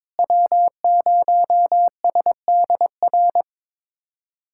Morse code, heard as a keyed tone, about 22 wpm, 700 Hz. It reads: W0SDR
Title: USA Callsign DE Callsign - 5 Characters - 22wpm